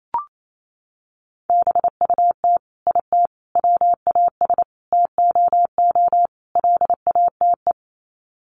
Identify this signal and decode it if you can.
Morse code, heard as a keyed tone, about 28 wpm, 700 Hz. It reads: BUT IT WAS TOO LATE
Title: Sets of 5 Words Encore2 28wpm